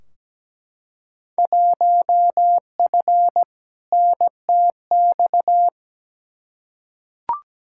Morse code, heard as a keyed tone, about 17 wpm, 700 Hz. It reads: 1F NTX
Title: Field Day Exchanges Encore - 17wpm